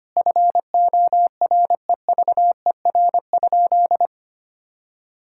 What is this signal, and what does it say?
Morse code, heard as a keyed tone, about 25 wpm, 700 Hz. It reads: FOREVER?
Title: Sentences from Top 1600 Words 25wpm